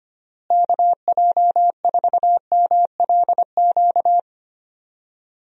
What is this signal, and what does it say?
Morse code, heard as a keyed tone, about 25 wpm, 700 Hz. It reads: KJ4MLQ
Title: USA Call Signs - 6 Characters - Repeated 3x - 25wpm